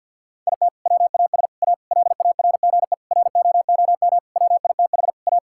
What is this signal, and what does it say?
Morse code, heard as a keyed tone, about 50 wpm, 700 Hz. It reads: IT WAS A LARGE ROOM WITH A
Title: Sentences from Top 1600 Words 50wpm